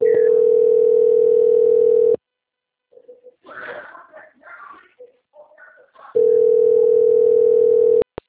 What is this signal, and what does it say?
Ringback tone (North America), 440 + 480 Hz.
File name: out- - - - - .13268